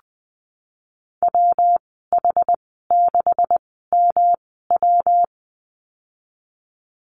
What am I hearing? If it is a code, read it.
Morse code, heard as a keyed tone, about 20 wpm, spaced as for 15 wpm, 700 Hz. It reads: WH6MW
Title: 154 Lesson 25g - Letter p - Call Signs 20wpm